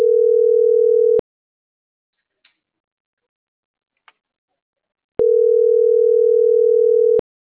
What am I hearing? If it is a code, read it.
Ringback tone (North America), 440 + 480 Hz.
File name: CENTRAL MULTIMODAL / out francisco.cobos